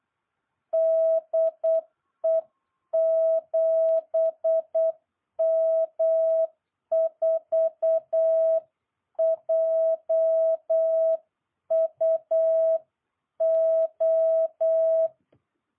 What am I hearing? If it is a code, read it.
Morse code, heard as a keyed tone, about 8 wpm, 660 Hz. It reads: DE 7M4JUO